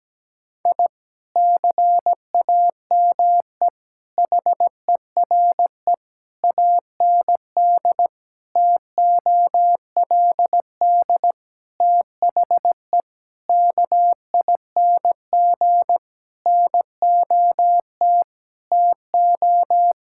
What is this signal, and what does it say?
Morse code, heard as a keyed tone, about 17 wpm, 700 Hz. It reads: I CAME HERE AND TOLD THE KING NOT TO
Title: Sentences from Top 1000 Words - Review Part 6 - 17wpm